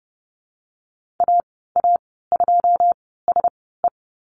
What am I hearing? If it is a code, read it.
Morse code, heard as a keyed tone, about 30 wpm, spaced as for 17 wpm, 700 Hz. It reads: AA2SE
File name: Lesson 15g - Number 5 - Call Signs 30wpm